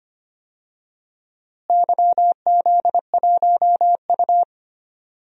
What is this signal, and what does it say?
Morse code, heard as a keyed tone, about 25 wpm, 700 Hz. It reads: YZ1U
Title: DXCC-100 Call Signs - 4 Characters - Repeated 2x - 25wpm